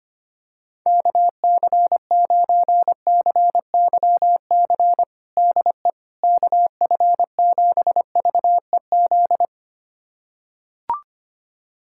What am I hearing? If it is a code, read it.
Morse code, heard as a keyed tone, about 25 wpm, 700 Hz. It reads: KC9CYC DE KF7VEZ
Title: USA Callsign DE Callsign 25wpm